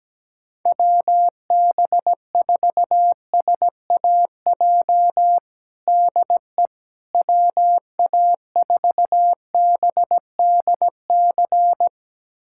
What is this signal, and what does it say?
Morse code, heard as a keyed tone, about 17 wpm, 700 Hz. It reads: WB4SAJ DE WA4BDC
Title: USA Callsign DE Callsign - 6 Characters - 17wpm